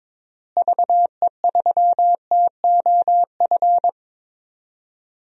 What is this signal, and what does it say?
Morse code, heard as a keyed tone, about 22 wpm, 700 Hz. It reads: VE3TOF
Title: DXCC-100 Call Signs - 6 Characters - Repeated 3x - 22wpm